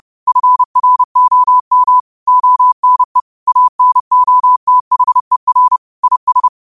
Morse code, heard as a keyed tone, about 30 wpm, 1000 Hz. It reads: FROM ONE ANOTHER IS